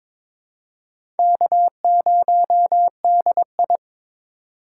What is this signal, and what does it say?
Morse code, heard as a keyed tone, about 22 wpm, 700 Hz. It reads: K0DI
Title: USA Call Signs - 4 Characters - 2x Character Spacing - 22wpm